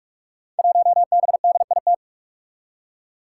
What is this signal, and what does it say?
Morse code, heard as a keyed tone, about 44 wpm, 700 Hz. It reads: 1BDIT
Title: Five Challenging Letter-Number Combinations 45wpm